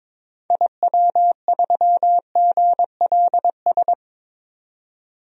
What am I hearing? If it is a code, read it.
Morse code, heard as a keyed tone, about 22 wpm, 700 Hz. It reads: IW3GLS
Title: DXCC-100 Call Signs - 6 Characters - 22wpm